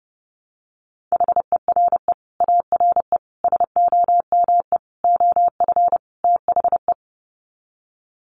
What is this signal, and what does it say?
Morse code, heard as a keyed tone, about 30 wpm, 700 Hz. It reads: HERE ARE SOME OF THE
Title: Sets of 5 Words Encore 30wpm